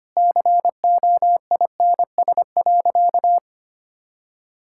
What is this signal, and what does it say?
Morse code, heard as a keyed tone, about 25 wpm, 700 Hz. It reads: COINS.
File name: Easy Sentences with Hints - 25wpm